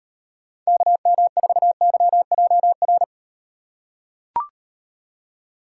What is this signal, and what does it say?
Morse code, heard as a keyed tone, about 38 wpm, 700 Hz. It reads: KM4YJR